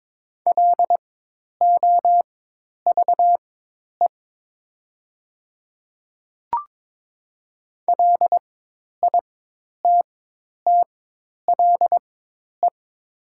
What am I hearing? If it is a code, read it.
Morse code, heard as a keyed tone, about 22 wpm, spaced as for 10 wpm, 700 Hz. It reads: LOVE LITTLE
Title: 4x Character Spacing - Top 500 Words - 22wpm